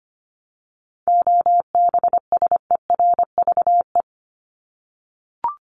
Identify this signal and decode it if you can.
Morse code, heard as a keyed tone, about 25 wpm, 700 Hz. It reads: OBSERVE
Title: ED Suffix - Top 100 Words with Base Words 25wpm